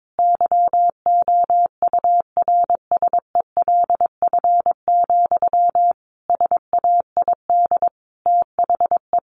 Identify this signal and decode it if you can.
Morse code, heard as a keyed tone, about 22 wpm, 700 Hz. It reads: YOURSELF, SAID THE
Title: Sentences from Top 1100 Words 22wpm